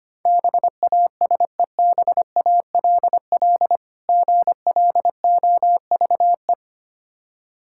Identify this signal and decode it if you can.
Morse code, heard as a keyed tone, about 25 wpm, 700 Hz. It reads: BASEBALL GLOVE
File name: practice / 25wpm / Difficult Words with Hints - 25wpm